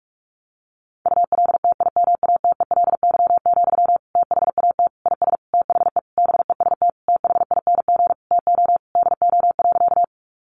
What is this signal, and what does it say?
Morse code, heard as a keyed tone, about 44 wpm, 700 Hz. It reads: ULTIMATELY, THAT IS THE BEST THING TO DO.
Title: Sentences from Top 1200 Words 45wpm